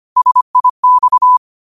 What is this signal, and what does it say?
Morse code, heard as a keyed tone, about 25 wpm, 1000 Hz. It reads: SIX
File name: qotdfiles / 25wpm